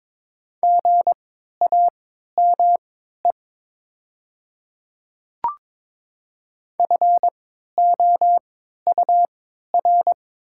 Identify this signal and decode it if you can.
Morse code, heard as a keyed tone, about 22 wpm, spaced as for 12 wpm, 700 Hz. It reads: GAME FOUR
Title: 3x Character Spacing - Top 500 Words - 22wpm